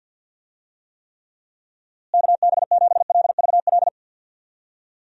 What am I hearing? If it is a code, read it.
Morse code, heard as a keyed tone, about 49 wpm, 700 Hz. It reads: KB7LVL
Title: USA Call Signs - 6 Characters - Repeated 3x - 50wpm